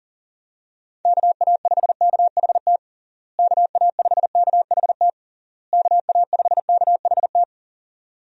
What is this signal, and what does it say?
Morse code, heard as a keyed tone, about 40 wpm, 700 Hz. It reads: KA5KHT KA5KHT KA5KHT
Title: USA Call Signs - 6 Characters - Repeated 3x - 40wpm